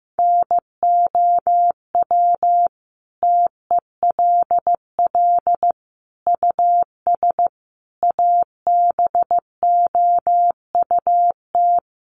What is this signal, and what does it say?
Morse code, heard as a keyed tone, about 15 wpm, 700 Hz. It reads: NOW TELL US ABOUT
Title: Sentences from Top 600 Words 15wpm